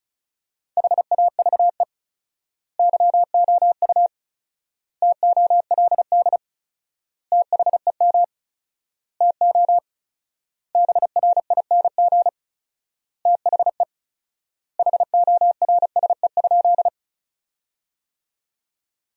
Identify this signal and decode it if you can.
Morse code, heard as a keyed tone, about 34 wpm, 700 Hz. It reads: HAVE YOU TOLD THEM TO BRING THE HORSE?
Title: 4x Word-Spacing - Sentences from Top 500 Words - 35wpm